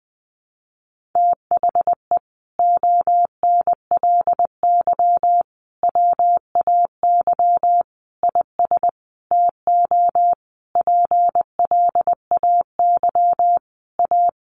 Morse code, heard as a keyed tone, about 20 wpm, 700 Hz. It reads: THE ONLY WAY IS TO PLAY A
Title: Sentences from Top 1500 Words 20wpm